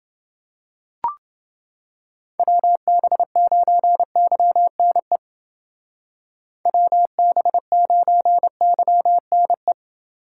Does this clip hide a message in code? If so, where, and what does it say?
Yes: Morse code from 2.4 to 9.7 s, heard as a keyed tone, about 28 wpm, 700 Hz. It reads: WB9YNE WB9YNE